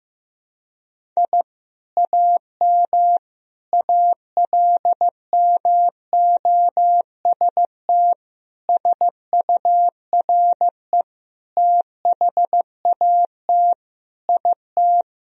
Morse code, heard as a keyed tone, about 16 wpm, 700 Hz. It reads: I AM ALMOST SURE THAT IT
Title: Sentences from Top 1000 Words - Review Part 4 - 15wpm